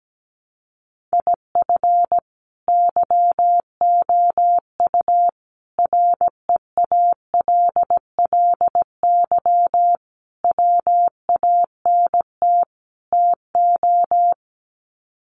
Morse code, heard as a keyed tone, about 17 wpm, 700 Hz. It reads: IF YOU REALLY WANT TO
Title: Sets of 5 Words Encore2 17wpm